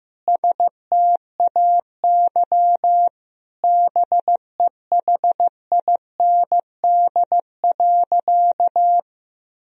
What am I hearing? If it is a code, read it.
Morse code, heard as a keyed tone, about 15 wpm, 700 Hz. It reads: STAY BEHIND.
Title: Sentences from Top 1000 Words - Review Part 1 - 15wpm